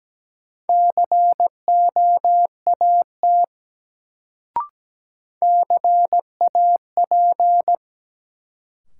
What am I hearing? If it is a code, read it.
Morse code, heard as a keyed tone, about 17 wpm, 700 Hz. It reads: COAT CAP
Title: Sentences from Top 1600 Words - New Word Drill Alt - 17wpm